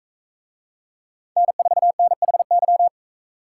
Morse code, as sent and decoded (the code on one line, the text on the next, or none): -. ....- -. .... -.--
N4NHY